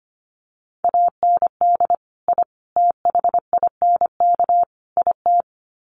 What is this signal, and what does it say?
Morse code, heard as a keyed tone, about 25 wpm, 700 Hz. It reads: AND I THINK IT
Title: Sets of 4 Words 25wpm